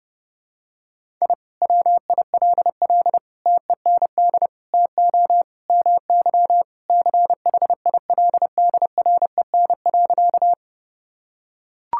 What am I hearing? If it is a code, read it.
Morse code, heard as a keyed tone, about 32 wpm, 700 Hz. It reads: I WILL TEND TO MY CHILDREN.
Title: Sentences from Top 1000 Words - Review Part 5 - 30wpm